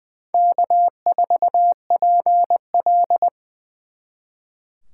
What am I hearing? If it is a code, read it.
Morse code, heard as a keyed tone, about 23 wpm, 700 Hz. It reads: K4PL
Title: Speed Racing - USA Call Signs 20wpm